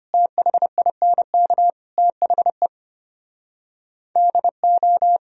Morse code, heard as a keyed tone, about 33 wpm, 700 Hz. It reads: THINK THE DO